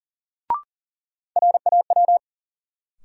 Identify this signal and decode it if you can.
Morse code, heard as a keyed tone, about 40 wpm, 700 Hz. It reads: RAW
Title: Sentences from Top 1500 Words - New Word Drill - 40wpm